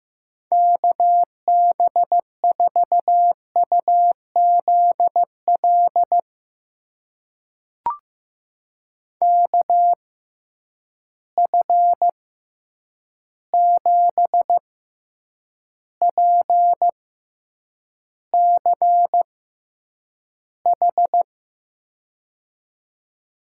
Morse code, heard as a keyed tone, about 15 wpm, spaced as for 8 wpm, 700 Hz. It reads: KB4UZL KF7PCH
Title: USA Call Signs - 6 Characters - 6x Character Spacing - 15wpm